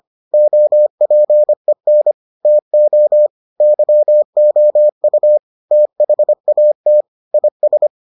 Morse code, heard as a keyed tone, about 25 wpm, 600 Hz. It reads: OPEN TO YOU THAT IS